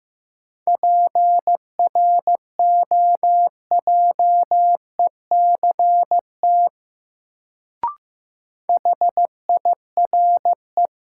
Morse code, heard as a keyed tone, about 15 wpm, 700 Hz. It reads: PROJECT HIRE
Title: Sentences from Top 1400 Words - New Word Drill - 15wpm